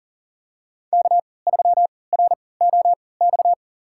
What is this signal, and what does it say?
Morse code, heard as a keyed tone, about 40 wpm, spaced as for 23 wpm, 700 Hz. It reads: K3ROX
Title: USA Call Signs - 3x Character Spacing - 40wpm